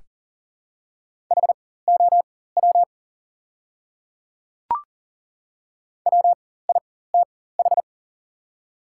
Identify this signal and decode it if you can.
Morse code, heard as a keyed tone, about 40 wpm, spaced as for 19 wpm, 700 Hz. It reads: HOW WITH
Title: 4x Character Spacing - Top 100 Words - 40wpm